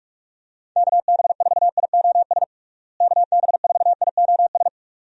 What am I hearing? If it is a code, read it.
Morse code, heard as a keyed tone, about 46 wpm, 700 Hz. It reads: KB4IOS KB4IOS
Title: USA Call Signs - 6 Characters - Repeated 2x - 45wpm